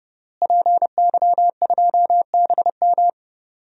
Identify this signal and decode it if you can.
Morse code, heard as a keyed tone, about 30 wpm, 700 Hz. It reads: PY2BM
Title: DXCC-100 Call Signs - 5 Characters - Repeated 3x - 30wpm